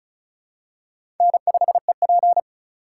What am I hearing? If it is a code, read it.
Morse code, heard as a keyed tone, about 34 wpm, 700 Hz. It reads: N5EP
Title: USA Call Signs - 4 Characters - 2x Character Spacing - 35wpm